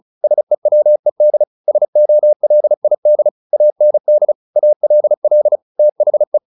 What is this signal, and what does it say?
Morse code, heard as a keyed tone, about 35 wpm, 600 Hz. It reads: SEWED SOLID AND ALL THE
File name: the old man and the sea / the old man and the sea 1 35wpm